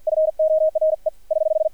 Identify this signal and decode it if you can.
Morse code, heard as a keyed tone, about 50 wpm, 630 Hz. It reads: FORE <HH>
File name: archived / CRSnet